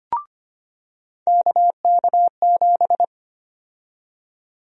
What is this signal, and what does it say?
Morse code, heard as a keyed tone, about 25 wpm, 700 Hz. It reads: KK7
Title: USA Call Sign Prefixes - 25wpm